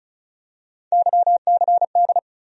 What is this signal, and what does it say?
Morse code, heard as a keyed tone, about 36 wpm, 700 Hz. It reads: YCD
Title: USA Call Sign Postfixes - 3 Letters 35wpm